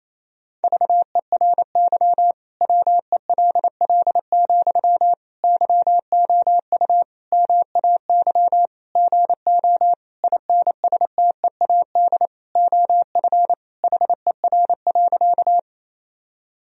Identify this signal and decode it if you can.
Morse code, heard as a keyed tone, about 28 wpm, 700 Hz. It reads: VERY WELL, YOU MAY GO INSTEAD OF HER.
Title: Sentences from Top 1000 Words - Review Part 4 - 28wpm